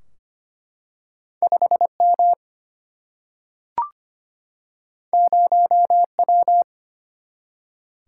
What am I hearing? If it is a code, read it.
Morse code, heard as a keyed tone, about 25 wpm, 700 Hz. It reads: 5 M 0 W
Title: Single-Letter Single-Number 25wpm